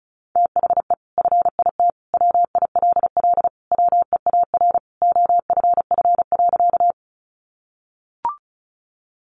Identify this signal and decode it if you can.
Morse code, heard as a keyed tone, about 34 wpm, 700 Hz. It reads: THE FIT WILL WEAR OFF.
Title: Sentences from Top 1500 Words 35wpm